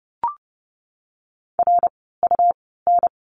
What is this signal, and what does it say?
Morse code, heard as a keyed tone, about 30 wpm, spaced as for 17 wpm, 700 Hz. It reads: RUN
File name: course / 30:10wpm / 115 Lesson 19f - Letter w - Words 30wpm